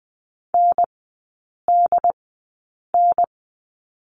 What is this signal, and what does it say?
Morse code, heard as a keyed tone, about 20 wpm, spaced as for 8 wpm, 700 Hz. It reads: NDN
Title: 087 Lesson 15e - Number 5 - Three Characters 20wpm